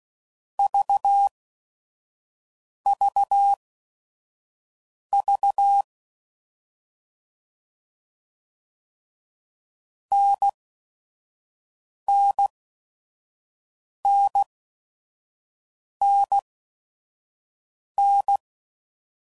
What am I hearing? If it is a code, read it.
Morse code, heard as a keyed tone, about 16 wpm, spaced as for 5 wpm, 800 Hz. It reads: VVV NNNNN
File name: Mixed / Short Runs / NDB NDB6